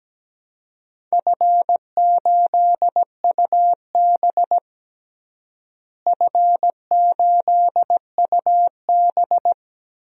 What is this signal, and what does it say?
Morse code, heard as a keyed tone, about 17 wpm, 700 Hz. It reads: F8UB F8UB
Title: DXCC-100 Call Signs - 4 Characters - Repeated 2x - 17wpm